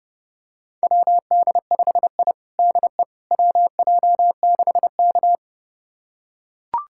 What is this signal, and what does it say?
Morse code, heard as a keyed tone, about 32 wpm, 700 Hz. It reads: WD5I DE WJ6K
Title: USA Callsign DE Callsign - 4 Characters - 30wpm